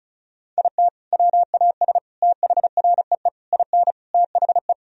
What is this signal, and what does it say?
Morse code, heard as a keyed tone, about 34 wpm, 700 Hz. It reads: IT WAS THREE IN THE
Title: Sentences from Top 1000 Words - Review Part 1 - 35wpm